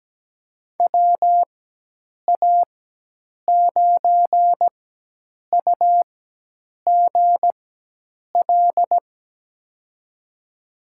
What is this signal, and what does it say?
Morse code, heard as a keyed tone, about 17 wpm, spaced as for 8 wpm, 700 Hz. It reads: WA9UGL
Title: USA Call Signs - 4x Character Spacing - 17wpm